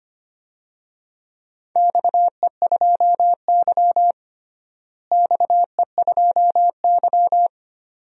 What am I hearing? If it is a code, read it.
Morse code, heard as a keyed tone, about 25 wpm, 700 Hz. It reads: XE2Y XE2Y
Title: DXCC-100 Call Signs - 4 Characters - Repeated 2x - 25wpm